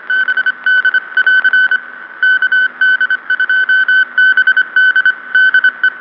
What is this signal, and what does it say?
Morse code, heard as a keyed tone, about 25 wpm, 1500 Hz. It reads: BD <AR> KD2BD DE